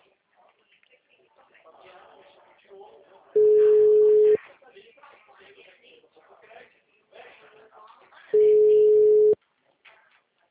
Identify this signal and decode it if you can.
Ringback tone (Europe, 425 Hz), with speech.
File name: out- -2004- - - .45079